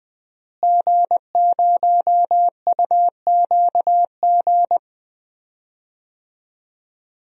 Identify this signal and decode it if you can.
Morse code, heard as a keyed tone, about 20 wpm, 700 Hz. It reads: G0UQG
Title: DXCC-100 Call Signs - 5 Characters - Repeated 2x - 20wpm